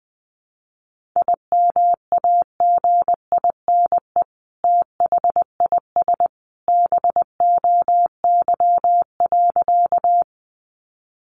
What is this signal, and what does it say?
Morse code, heard as a keyed tone, about 20 wpm, 700 Hz. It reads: IMAGINE THIS BOY.
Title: Sentences from Top 1400 Words 20wpm